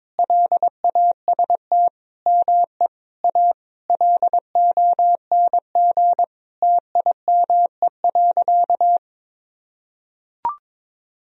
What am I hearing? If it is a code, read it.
Morse code, heard as a keyed tone, about 22 wpm, 700 Hz. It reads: LAST ME A LONG TIME.
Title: Sentences from Top 1000 Words - Review Part 2 - 22wpm